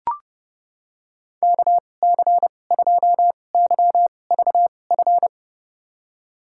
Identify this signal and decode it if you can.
Morse code, heard as a keyed tone, about 30 wpm, spaced as for 22 wpm, 700 Hz. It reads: KC2YVF
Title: USA Call Signs - 6 Characters - 2x Character Spacing - 30wpm